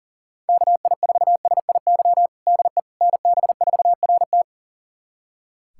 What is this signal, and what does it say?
Morse code, heard as a keyed tone, about 40 wpm, 700 Hz. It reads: KI4SIY DE NB4RT
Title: USA Callsign DE Callsign 40wpm